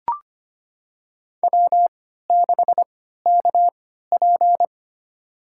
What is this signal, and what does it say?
Morse code, heard as a keyed tone, about 25 wpm, spaced as for 14 wpm, 700 Hz. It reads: W6KP
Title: USA Call Signs - 4 Characters - 3x Character Spacing - 25wpm